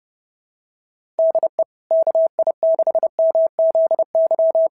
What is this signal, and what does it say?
Morse code, heard as a keyed tone, about 26 wpm, 660 Hz. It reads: DE KI6MZY
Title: SOTA QSO Practice 30wpm